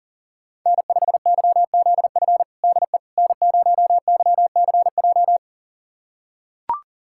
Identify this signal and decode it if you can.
Morse code, heard as a keyed tone, about 40 wpm, 700 Hz. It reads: N5YZF DE N0YCJ
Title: USA Callsign DE Callsign - 5 Characters - 40wpm